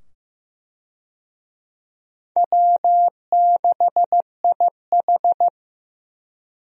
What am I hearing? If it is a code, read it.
Morse code, heard as a keyed tone, about 15 wpm, 700 Hz. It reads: W6IH
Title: USA Call Signs - 4 Characters - 15wpm